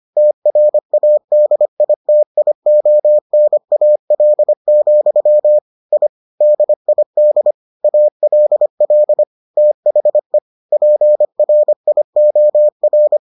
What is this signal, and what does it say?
Morse code, heard as a keyed tone, about 25 wpm, 600 Hz. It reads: TRADITIONAL, I DID ALL THE PRIOR